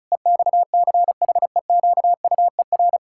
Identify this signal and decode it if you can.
Morse code, heard as a keyed tone, about 36 wpm, 700 Hz. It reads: EXCHEQUER
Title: Pangram Challenge v2 - 35wpm